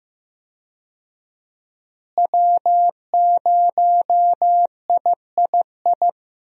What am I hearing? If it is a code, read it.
Morse code, heard as a keyed tone, about 15 wpm, 700 Hz. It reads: W0III
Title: USA Call Signs - 5 Characters - Repeated 3x - 15wpm